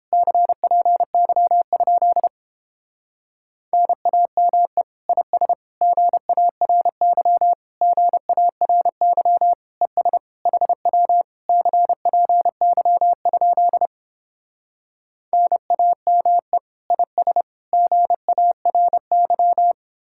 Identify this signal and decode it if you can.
Morse code, heard as a keyed tone, about 29 wpm, 700 Hz. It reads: CPY? NAME IS GARY GARY ES HW CPY? NAME IS GARY